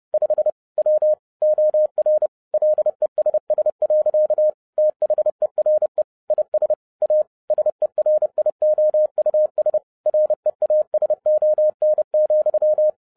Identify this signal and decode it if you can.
Morse code, heard as a keyed tone, about 30 wpm, 610 Hz. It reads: 5 W OR LESS. THERE IS A SERIOUS REASON,
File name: 2011-07-mp3 / n40 EXCITING QRP CW OPERATION 30wpm